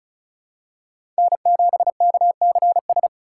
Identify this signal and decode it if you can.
Morse code, heard as a keyed tone, about 36 wpm, 700 Hz. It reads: N7KCS
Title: USA Call Signs - 4x Character Spacing - 35wpm